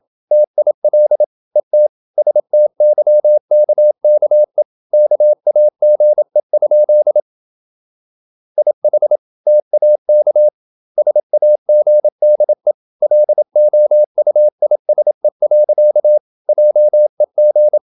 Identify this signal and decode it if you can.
Morse code, heard as a keyed tone, about 27 wpm, 600 Hz. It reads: TIL ET STYKKE KAGE? IH TAK SAGDE LOUISE. JEG
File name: Ferie QRQ 27wpm